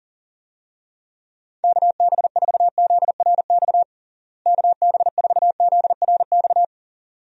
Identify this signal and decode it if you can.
Morse code, heard as a keyed tone, about 40 wpm, 700 Hz. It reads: KB4ZRX KB4ZRX
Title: USA Call Signs - 6 Characters - Repeated 2x - 40wpm